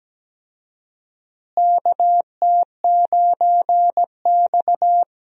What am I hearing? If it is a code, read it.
Morse code, heard as a keyed tone, about 17 wpm, 700 Hz. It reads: KT9X
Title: USA Call Signs - 4 Characters - Repeated 2x - 17wpm